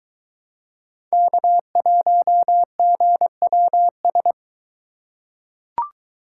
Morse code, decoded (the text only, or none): K1GWS